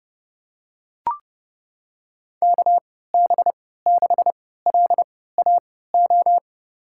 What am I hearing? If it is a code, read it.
Morse code, heard as a keyed tone, about 31 wpm, spaced as for 17 wpm, 700 Hz. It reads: KB6LAO
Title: USA Call Signs - 6 Characters - 3x Character Spacing - 30wpm